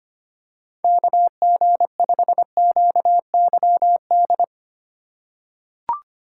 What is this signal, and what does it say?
Morse code, heard as a keyed tone, about 25 wpm, 700 Hz. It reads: KG5QYD
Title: USA Call Signs - 4x Character Spacing - 25wpm